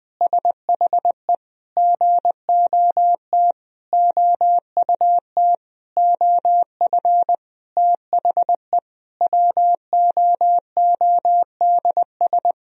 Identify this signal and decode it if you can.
Morse code, heard as a keyed tone, about 20 wpm, 700 Hz. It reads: SHE GOT OUT OF THE WOODS
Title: Sentences from Top 1000 Words - Review Part 2 - 20wpm